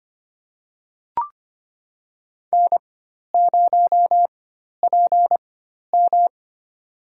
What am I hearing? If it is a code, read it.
Morse code, heard as a keyed tone, about 25 wpm, spaced as for 12 wpm, 700 Hz. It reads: N0PM
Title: USA Call Signs - 4x Character Spacing - 25wpm